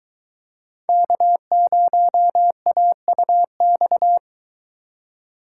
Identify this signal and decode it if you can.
Morse code, heard as a keyed tone, about 22 wpm, 700 Hz. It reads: K0AUX